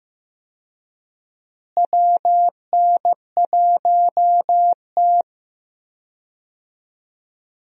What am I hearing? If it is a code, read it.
Morse code, heard as a keyed tone, about 15 wpm, 700 Hz. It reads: WN1T
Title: USA Call Signs - 4 Characters - Repeated 3x - 15wpm